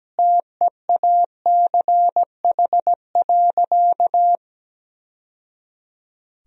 Morse code, heard as a keyed tone, about 17 wpm, 700 Hz. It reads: TEACH.
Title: Sentences from Top 1000 Words - Review Part 2 - 17wpm